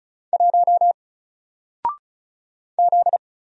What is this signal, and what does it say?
Morse code, heard as a keyed tone, about 36 wpm, 700 Hz. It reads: 1 Z
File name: Single-Letter-Number-35wpm